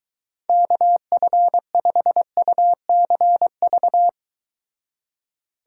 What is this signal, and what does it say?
Morse code, heard as a keyed tone, about 23 wpm, 700 Hz. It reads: KF5UCV